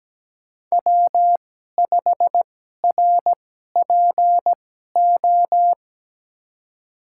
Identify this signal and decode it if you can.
Morse code, heard as a keyed tone, about 17 wpm, spaced as for 12 wpm, 700 Hz. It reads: W5RPO
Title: USA Call Signs - 5 Characters - 2x Character Spacing - 17wpm